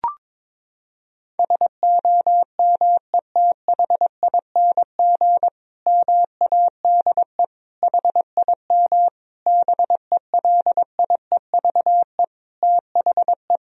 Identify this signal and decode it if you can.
Morse code, heard as a keyed tone, about 21 wpm, 700 Hz. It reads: SOMETHING MADE HIM BELIEVE THE
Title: Sentences from Top 1000 Words - Review Part 1 - 22wpm